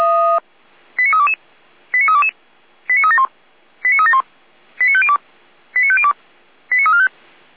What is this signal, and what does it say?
Five-tone selective call (ZVEI), heard as quick runs of five tones, tones of about 70 ms. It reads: 89322 89322 89371 89371 89502 89502 89345
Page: SDR/ZVEI